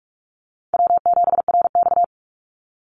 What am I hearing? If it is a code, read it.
Morse code, heard as a keyed tone, about 46 wpm, 700 Hz. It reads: W7RX
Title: Call Signs - 45wpm